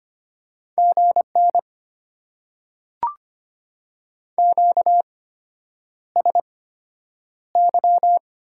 Morse code, heard as a keyed tone, about 25 wpm, spaced as for 9 wpm, 700 Hz. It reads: GN QSY
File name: Top 90 QSO Elements - 8x Character Spacing - 25wpm